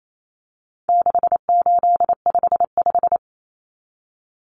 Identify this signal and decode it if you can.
Morse code, heard as a keyed tone, about 28 wpm, 700 Hz. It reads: 6855
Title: Common Cut Numbers - 4-Digit 28wpm